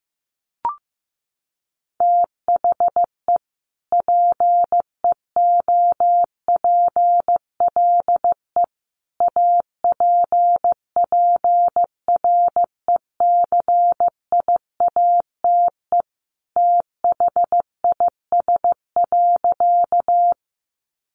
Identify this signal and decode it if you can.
Morse code, heard as a keyed tone, about 15 wpm, 700 Hz. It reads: THE PEOPLE APPRECIATE THIS.
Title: Sentences from Top 1300 Words 15wpm